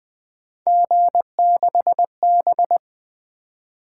Morse code, heard as a keyed tone, about 20 wpm, 700 Hz. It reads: G6B
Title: Three Letter-Number Combinations - 20wpm